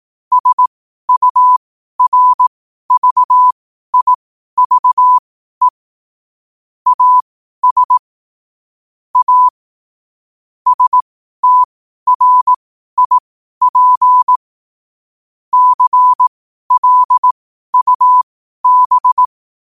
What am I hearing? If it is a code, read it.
Morse code, heard as a keyed tone, about 18 wpm, spaced as for 12 wpm, 1000 Hz. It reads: SURVIVE AS A STRIP CLUB
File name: qotdfiles / 12wpm